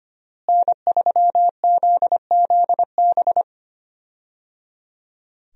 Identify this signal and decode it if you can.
Morse code, heard as a keyed tone, about 25 wpm, 700 Hz. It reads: N3ZZB
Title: USA Call Signs - 5 Characters - Repeated 3x - 25wpm